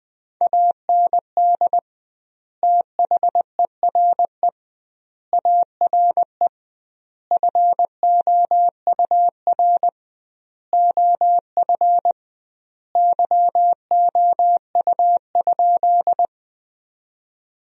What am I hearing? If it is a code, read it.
Morse code, heard as a keyed tone, about 20 wpm, 700 Hz. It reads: AND THERE ARE FOUR OF YOU?
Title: 2x Word-Spacing - Sentences from Top 300 Words - 20wpm